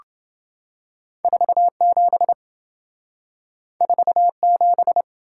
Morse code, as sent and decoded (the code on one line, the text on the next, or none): ....- --... / ....- --...
47 47